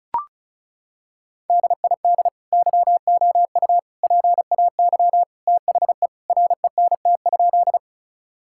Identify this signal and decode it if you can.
Morse code, heard as a keyed tone, about 36 wpm, 700 Hz. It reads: DID YOU PAY THE RENT?
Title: Sentences from Top 1000 Words - Review Part 5 - 35wpm